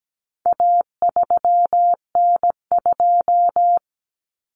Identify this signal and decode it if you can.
Morse code, heard as a keyed tone, about 17 wpm, 700 Hz. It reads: A3N2
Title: Common Cut Numbers - 4-Digit 17wpm